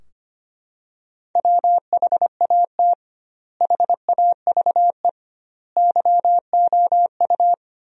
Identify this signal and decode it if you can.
Morse code, heard as a keyed tone, about 25 wpm, 700 Hz. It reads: WHAT HAVE YOU
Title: 2x Word-Spacing - Sentences from Top 400 Words - 25wpm